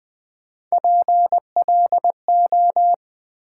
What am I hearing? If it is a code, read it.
Morse code, heard as a keyed tone, about 20 wpm, 700 Hz. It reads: PLO
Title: Build A Word - 1 Letter At A Time - 4-Letter Words - 20wpm